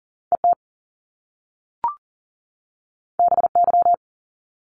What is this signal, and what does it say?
Morse code, heard as a keyed tone, about 40 wpm, 700 Hz. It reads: ET BY
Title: Top Two Letter Combinations in Words 40wpm